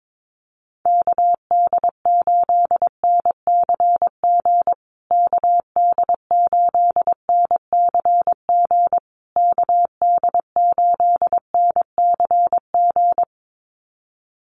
Morse code, heard as a keyed tone, about 22 wpm, 700 Hz. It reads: KD8NCG KD8NCG KD8NCG
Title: USA Call Signs Repeated 3x 22wpm